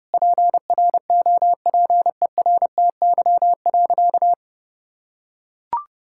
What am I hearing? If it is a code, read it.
Morse code, heard as a keyed tone, about 30 wpm, 700 Hz. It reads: PROPERTY.
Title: Sentences from Top 1600 Words 30wpm